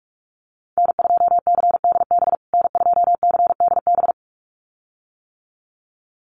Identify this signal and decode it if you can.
Morse code, heard as a keyed tone, about 44 wpm, 690 Hz. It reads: N2CDB N2CDB
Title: USA Call Signs Repeated 2x 45wpm